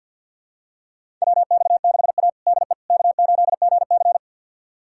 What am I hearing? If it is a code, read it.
Morse code, heard as a keyed tone, about 50 wpm, 700 Hz. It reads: WX6A DE K8GC
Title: USA Callsign DE Callsign - 4 Characters - 50wpm